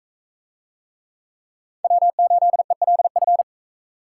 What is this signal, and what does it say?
Morse code, heard as a keyed tone, about 43 wpm, 700 Hz. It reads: W8ELF